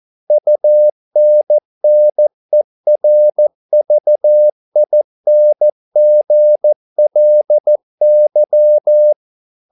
Morse code, heard as a keyed tone, about 14 wpm, 600 Hz. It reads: UNNERVINGLY